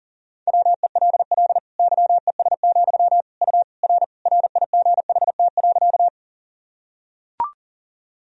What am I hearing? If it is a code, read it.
Morse code, heard as a keyed tone, about 40 wpm, 700 Hz. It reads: WELL YES, U R RIGHT.
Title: Sentences from Top 500 Words and Abbreviations 40wpm